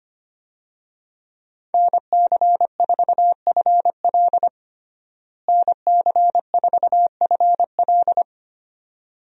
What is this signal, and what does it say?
Morse code, heard as a keyed tone, about 25 wpm, 700 Hz. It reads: NC4FL NC4FL
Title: USA Call Signs - 5 Characters - Repeated 2x - 25wpm